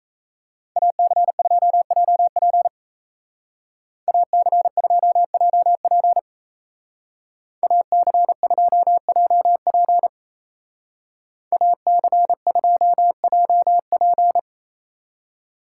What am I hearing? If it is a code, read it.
Morse code, heard as a keyed tone, about 34 wpm, 700 Hz. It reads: AC2JP AC2JP AC2JP AC2JP
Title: Speed Racing - USA 2x2 Call Signs 28wpm